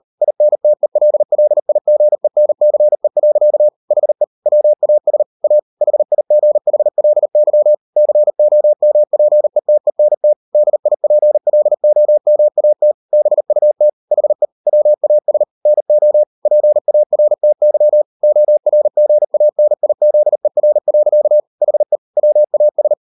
Morse code, heard as a keyed tone, about 38 wpm, 600 Hz. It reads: INTELLIGENCE. HE WAS A HIGHLY COMPETENT DIPLOMAT BUT HE WAS NO PARTY ORGANIZER. HE WAS